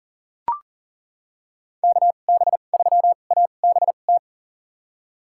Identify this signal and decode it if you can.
Morse code, heard as a keyed tone, about 40 wpm, spaced as for 29 wpm, 700 Hz. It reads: KB3ABT
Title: USA Call Signs - 6 Characters - 2x Character Spacing - 40wpm